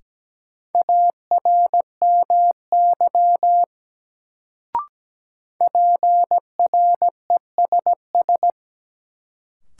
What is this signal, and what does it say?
Morse code, heard as a keyed tone, about 17 wpm, 700 Hz. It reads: ARMY PRESS
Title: Sentences from Top 800 Words - New Word Drill - 17wpm